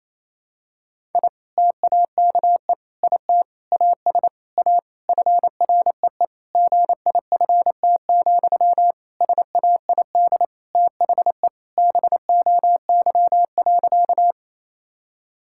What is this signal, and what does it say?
Morse code, heard as a keyed tone, about 28 wpm, 700 Hz. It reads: I TAKE IT AS A FREE GIFT, SAID THE BOY.
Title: Sentences from Top 1000 Words - Review Part 6 - 28wpm